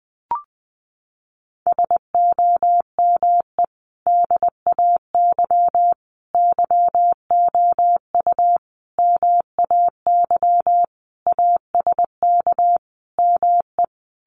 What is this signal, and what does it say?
Morse code, heard as a keyed tone, about 20 wpm, 700 Hz. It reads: SOME DAY YOU MAY ASK ME
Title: Sentences from Top 200 Words 20wpm